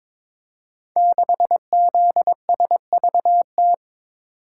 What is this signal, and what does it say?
Morse code, heard as a keyed tone, about 22 wpm, 700 Hz. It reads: 6ZSVT
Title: Code Groups - 5 Letter:Numbers - 22wpm